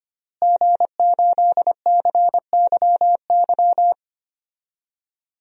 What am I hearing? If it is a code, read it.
Morse code, heard as a keyed tone, about 25 wpm, 700 Hz. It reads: G8CYY
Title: DXCC-100 Call Signs - 5 Characters - Repeated 3x - 25wpm